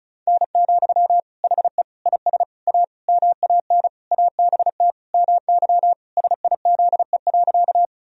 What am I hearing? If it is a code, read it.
Morse code, heard as a keyed tone, about 36 wpm, 700 Hz. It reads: N, HE IS A MAN ABT MY SIZE.
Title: Sentences from Top 400 Words and Abbreviations 35wpm